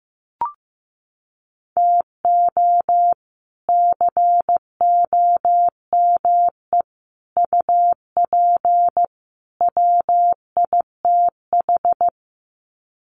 Morse code, heard as a keyed tone, about 15 wpm, 700 Hz. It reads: TO COME UP WITH
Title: Sets of 4 Words 15wpm